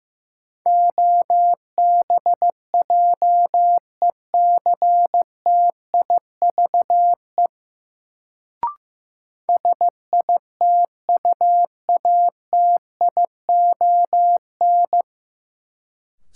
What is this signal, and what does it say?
Morse code, heard as a keyed tone, about 15 wpm, 700 Hz. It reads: OBJECTIVE SITUATION
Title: Sentences from Top 1200 Words - New Word Drill - 15wpm